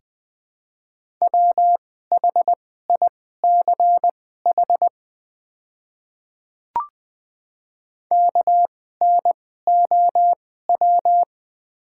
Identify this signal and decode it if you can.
Morse code, heard as a keyed tone, about 20 wpm, spaced as for 14 wpm, 700 Hz. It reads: WHICH KNOW
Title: 2x Character Spacing - Top 200 Words - 20wpm